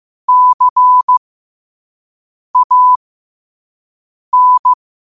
Morse code, heard as a keyed tone, about 15 wpm, spaced as for 5 wpm, 1000 Hz. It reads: CAN
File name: qotdfiles / 5wpm